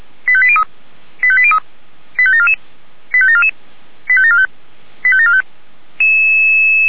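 Five-tone selective call (ZVEI), heard as quick runs of five tones, tones of about 70 ms. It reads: 85803 85803 85744 85744 85745 85745
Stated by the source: ZVEI five-tone selective call 85827, received by SDR.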